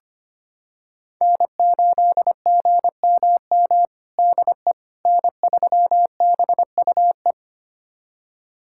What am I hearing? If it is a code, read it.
Morse code, heard as a keyed tone, about 25 wpm, 700 Hz. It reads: N8GMM DE N3BUE
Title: USA Callsign DE Callsign - 5 Characters - 25wpm